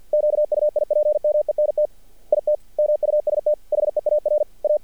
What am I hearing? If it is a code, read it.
Morse code, heard as a keyed tone, about 54 wpm, 600 Hz. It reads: QUIPMENT IT MUST HEAR A
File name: archived / CRSnet